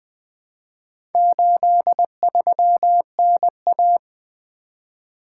Morse code, heard as a keyed tone, about 20 wpm, 700 Hz. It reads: 83NA
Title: Four Letter-Number Combinations - 20wpm